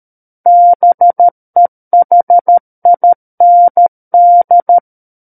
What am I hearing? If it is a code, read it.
Morse code, heard as a keyed tone, about 13 wpm, 700 Hz. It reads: BEHIND
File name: Book One / 04-Chapter 4 - 13 WPM